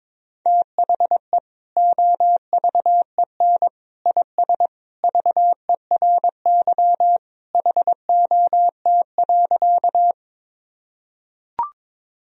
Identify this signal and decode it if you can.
Morse code, heard as a keyed tone, about 22 wpm, 700 Hz. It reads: THE OVEN IS VERY HOT.
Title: Sentences from Top 1000 Words - Review Part 4 - 22wpm